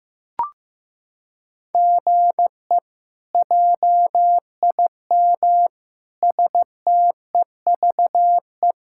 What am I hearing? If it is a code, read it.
Morse code, heard as a keyed tone, about 16 wpm, 700 Hz. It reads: GE JIM STEVE
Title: K1USN SST QSO Practice - 15wpm